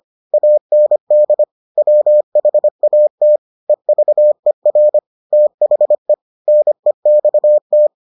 Morse code, heard as a keyed tone, about 25 wpm, 600 Hz. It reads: AND WHAT EVER THE NEXT